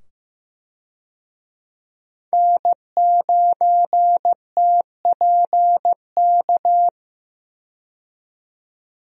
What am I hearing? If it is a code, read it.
Morse code, heard as a keyed tone, about 15 wpm, 700 Hz. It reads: N9TPK
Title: USA Call Signs - 5 Characters - 15wpm